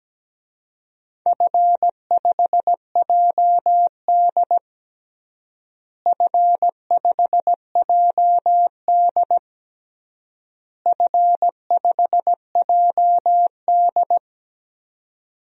Morse code, heard as a keyed tone, about 17 wpm, 700 Hz. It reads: F5JD F5JD F5JD
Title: DXCC-100 Call Signs - 4 Characters - Repeated 3x - 17wpm